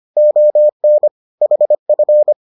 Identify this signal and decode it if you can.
Morse code, heard as a keyed tone, about 25 wpm, 600 Hz. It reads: ON HF